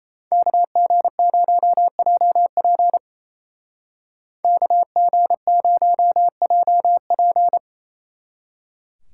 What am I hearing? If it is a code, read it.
Morse code, heard as a keyed tone, about 33 wpm, 700 Hz. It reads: KG0JP KG0JP
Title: Speed Racing - USA 2x2 Call Signs 28wpm